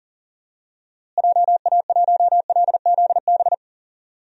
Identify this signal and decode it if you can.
Morse code, heard as a keyed tone, about 40 wpm, 700 Hz. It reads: JA1LZB
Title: DXCC-100 Call Signs - 6 Characters - Repeated 2x - 40wpm